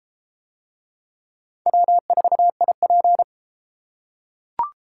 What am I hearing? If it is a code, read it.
Morse code, heard as a keyed tone, about 32 wpm, 700 Hz. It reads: W4IP